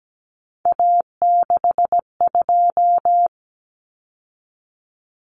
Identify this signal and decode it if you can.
Morse code, heard as a keyed tone, about 17 wpm, 700 Hz. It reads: A62
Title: Common Cut Numbers - 3-Digit 17wpm